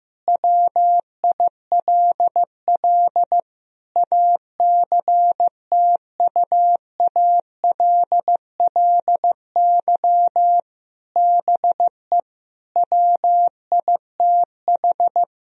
Morse code, heard as a keyed tone, about 16 wpm, 700 Hz. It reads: WILL ACTUALLY BE WITH
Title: Sentences from Top 1000 Words - Review Part 4 - 15wpm